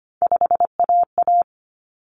Morse code, heard as a keyed tone, about 25 wpm, 700 Hz. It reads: 5AA
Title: Common Cut Numbers - 3-Digit 25wpm